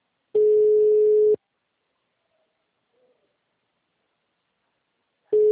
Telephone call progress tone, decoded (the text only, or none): ringback tone